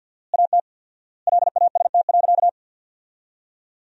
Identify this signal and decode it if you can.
Morse code, heard as a keyed tone, about 51 wpm, 700 Hz. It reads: AT LAST.
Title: 4x Word-Spacing - Sentences from Top 300 Words - 50wpm